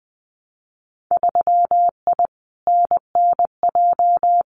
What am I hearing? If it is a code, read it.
Morse code, heard as a keyed tone, about 20 wpm, 700 Hz. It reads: 3I NNJ
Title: Winter Field Day Exchanges 20wpm